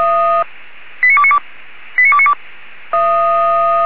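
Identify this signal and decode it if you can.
Five-tone selective call (ZVEI), heard as quick runs of five tones, tones of about 70 ms. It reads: 89292 89292
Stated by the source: ZVEI five-tone call received on SDR, 464 channel.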